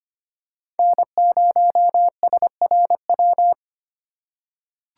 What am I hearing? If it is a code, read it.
Morse code, heard as a keyed tone, about 25 wpm, 700 Hz. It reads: N0SRW
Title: USA Call Signs - 5 Characters - 4x Character Spacing - 25wpm